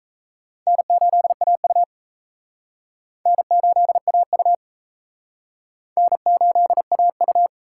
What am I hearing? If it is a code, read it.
Morse code, heard as a keyed tone, about 34 wpm, 700 Hz. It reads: N8AU N8AU N8AU
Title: Speed Racing - USA 1x2 Call Signs 28wpm